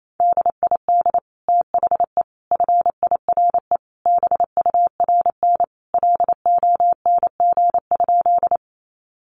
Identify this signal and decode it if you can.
Morse code, heard as a keyed tone, about 28 wpm, 700 Hz. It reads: DID THE FIRE BURN LONG?
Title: Sentences from Top 1300 Words 28wpm